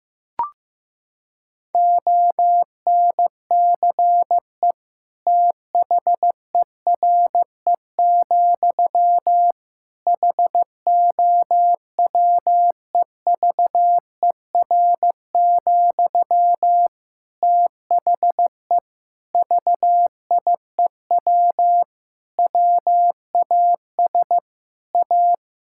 Morse code, heard as a keyed tone, about 15 wpm, 700 Hz. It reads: ONCE THERE, HOWEVER, THE VIEW WAS A
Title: Sentences from Top 1000 Words - Review Part 6 - 15wpm